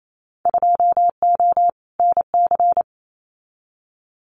Morse code, heard as a keyed tone, about 28 wpm, 700 Hz. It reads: 2O NC
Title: Winter Field Day Exchanges 28wpm